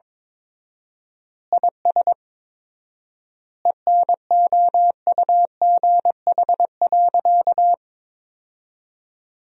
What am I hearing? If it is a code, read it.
Morse code, heard as a keyed tone, about 22 wpm, 700 Hz. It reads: IS ENOUGH.
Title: 4x Word-Spacing - Sentences from Top 400 Words - 22wpm